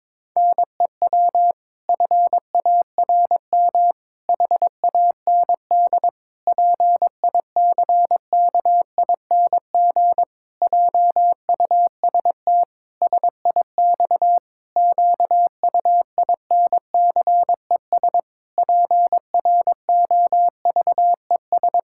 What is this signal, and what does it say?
Morse code, heard as a keyed tone, about 22 wpm, 700 Hz. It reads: NEW FARM HAND PICKING JUST SIX QUINCES PROVES
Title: Pangram Challenge v2 - 22wpm